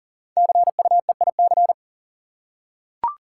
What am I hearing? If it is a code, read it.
Morse code, heard as a keyed tone, about 40 wpm, 700 Hz. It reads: CUEIC
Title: Code Groups - 5 Letter:Numbers - 40wpm